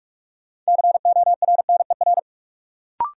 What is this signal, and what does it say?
Morse code, heard as a keyed tone, about 47 wpm, 700 Hz. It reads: CORNER
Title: Sentences from Top 1600 Words - New Word Drill - 45wpm